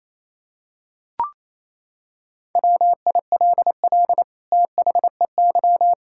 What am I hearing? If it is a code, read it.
Morse code, heard as a keyed tone, about 28 wpm, 700 Hz. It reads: WILL THEY
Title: Sentences from Top 1000 Words - Review Part 6 - 28wpm